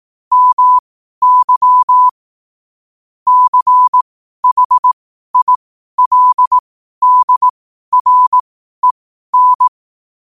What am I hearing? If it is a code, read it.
Morse code, heard as a keyed tone, about 18 wpm, spaced as for 12 wpm, 1000 Hz. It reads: MY CHILDREN
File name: qotdfiles / 12wpm